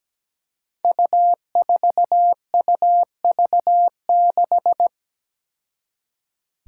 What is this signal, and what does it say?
Morse code, heard as a keyed tone, about 17 wpm, 700 Hz. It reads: U4UV6
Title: Five Challenging Letter-Number Combinations 17wpm